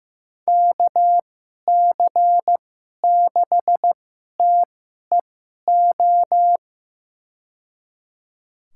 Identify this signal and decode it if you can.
Morse code, heard as a keyed tone, about 15 wpm, spaced as for 11 wpm, 700 Hz. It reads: KC6TEO
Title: USA Call Signs - 6 Characters - 2x Character Spacing - 15wpm